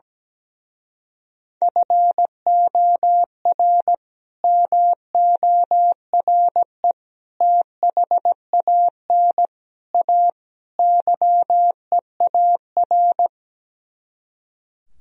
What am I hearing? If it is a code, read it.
Morse code, heard as a keyed tone, about 17 wpm, 700 Hz. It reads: FOR MORE THAN A YEAR
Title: Speed-Racing - Sets of 5 Words 17wpm